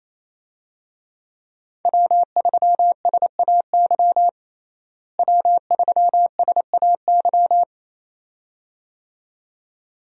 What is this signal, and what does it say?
Morse code, heard as a keyed tone, about 28 wpm, 700 Hz. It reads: W3SAY W3SAY
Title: USA Call Signs - 5 Characters - Repeated 2x - 28wpm